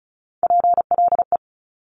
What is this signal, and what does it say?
Morse code, heard as a keyed tone, about 36 wpm, 700 Hz. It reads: PLE
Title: Top Three Letter Combinations in Words 35wpm